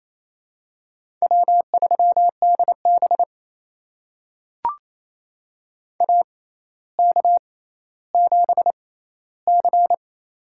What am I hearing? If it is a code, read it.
Morse code, heard as a keyed tone, about 28 wpm, spaced as for 14 wpm, 700 Hz. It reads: W3DB AK7C
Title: USA Call Signs - 4 Characters - 6x Character Spacing - 28wpm